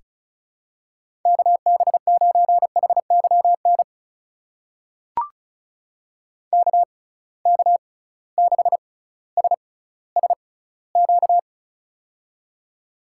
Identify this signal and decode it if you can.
Morse code, heard as a keyed tone, about 35 wpm, spaced as for 18 wpm, 700 Hz. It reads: KB9HYN KK6SSQ
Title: USA Call Signs - 6 Characters - 6x Character Spacing - 35wpm